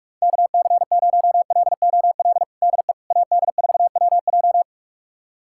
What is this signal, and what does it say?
Morse code, heard as a keyed tone, about 41 wpm, 700 Hz. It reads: KC0LOL DE AD4WJ
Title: USA Callsign DE Callsign 45wpm